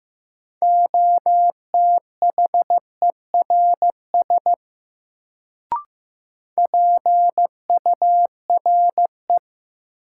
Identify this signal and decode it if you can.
Morse code, heard as a keyed tone, about 15 wpm, 700 Hz. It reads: OTHERS PURE
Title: Sentences from Top 1500 Words - New Word Drill Alt - 15wpm